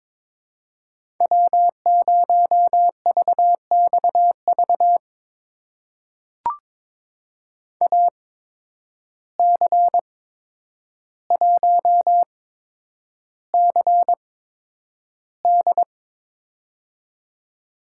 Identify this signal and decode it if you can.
Morse code, heard as a keyed tone, about 22 wpm, spaced as for 9 wpm, 700 Hz. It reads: W0VXV AC1CD
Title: USA Call Signs - 5 Characters - 8x Character Spacing - 22wpm